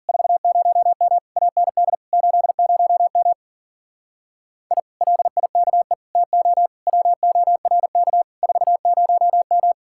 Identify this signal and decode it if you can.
Morse code, heard as a keyed tone, about 54 wpm, 700 Hz. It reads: 40M AND 80M I LIKE TO WORK 40M